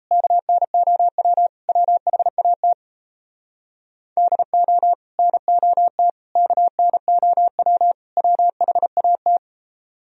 Morse code, heard as a keyed tone, about 37 wpm, 700 Hz. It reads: KNOW WHAT DO NOT KNOW WHAT